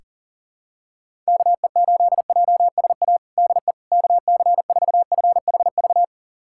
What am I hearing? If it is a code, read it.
Morse code, heard as a keyed tone, about 40 wpm, 700 Hz. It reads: KE8JSA DE KC4FHV
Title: USA Callsign DE Callsign - 6 Characters - 40wpm